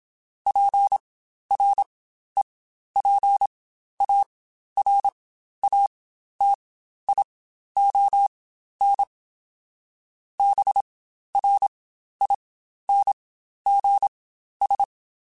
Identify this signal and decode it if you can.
Morse code, heard as a keyed tone, about 27 wpm, spaced as for 12 wpm, 800 Hz. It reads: PREPARATION BRINGS
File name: Quotes Spaced / Quotes Spaced-mp3s / Persperation sp @25